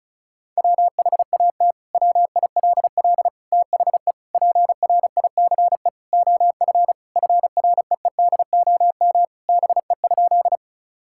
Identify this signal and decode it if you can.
Morse code, heard as a keyed tone, about 36 wpm, 700 Hz. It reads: WHAT WILL THE PRICE OF FREEDOM BE?
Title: Sentences from Top 1000 Words - Review Part 5 - 35wpm